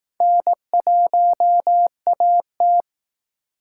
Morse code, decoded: N1AT

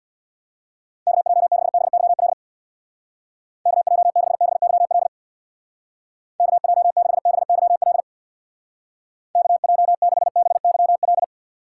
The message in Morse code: KJ6BYL KJ6BYL KJ6BYL KJ6BYL

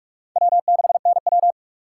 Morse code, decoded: W6NW

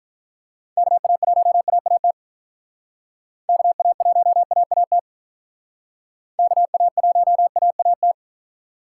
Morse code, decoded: KA1AAT KA1AAT KA1AAT

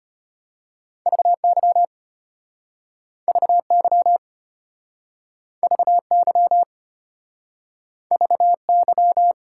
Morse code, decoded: VY VY VY VY

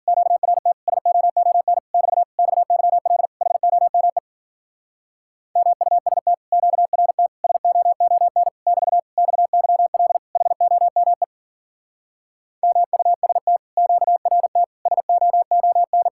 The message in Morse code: QRT SOON <BT> XYL HOME MUST QRT SOON <BT> XYL HOME MUST QRT SOON

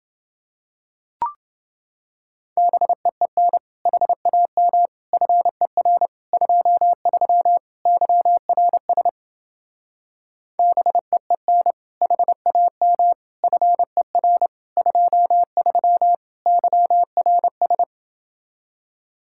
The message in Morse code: BEEN HAM FER 23 YRS BEEN HAM FER 23 YRS

